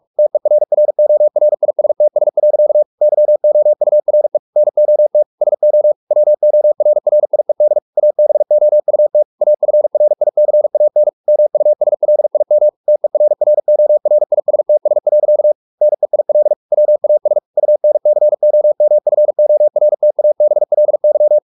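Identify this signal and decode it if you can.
Morse code, heard as a keyed tone, about 44 wpm, 600 Hz. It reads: TERRORISTS. YOURE NOT SO WORRIED ABOUT AFRICAN MUSLIM TERRORISTS. NEIL WAS UNCOMFORTABLY
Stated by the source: Morse practice at 45 wpm, the novel Never by Ken Follett in English, from CW med Gnister.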